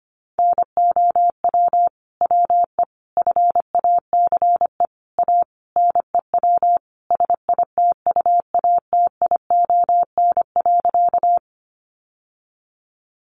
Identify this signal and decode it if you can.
Morse code, heard as a keyed tone, about 25 wpm, 700 Hz. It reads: NOW WE FACE A NEW SITUATION.
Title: Sentences from Top 1200 Words 25wpm